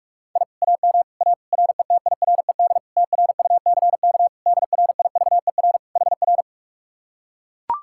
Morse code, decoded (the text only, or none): I AM A RETIRED TRUCK DRIVER HR